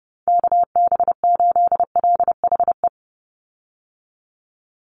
Morse code, heard as a keyed tone, about 30 wpm, 700 Hz. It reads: KB8LHE
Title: USA Call Signs 30wpm